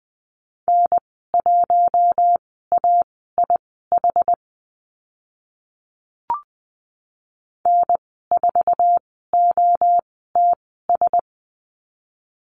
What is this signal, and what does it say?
Morse code, heard as a keyed tone, about 20 wpm, spaced as for 14 wpm, 700 Hz. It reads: N1AIH N4OTS
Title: 076 Lesson 13g - Letter L - Call Signs 20wpm